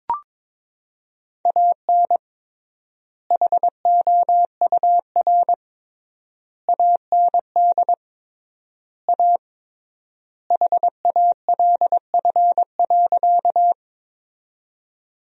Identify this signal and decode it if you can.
Morse code, heard as a keyed tone, about 22 wpm, 700 Hz. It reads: AN HOUR AND A HALF.
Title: 3x Word-Spacing - Sentences from Top 400 Words - 22wpm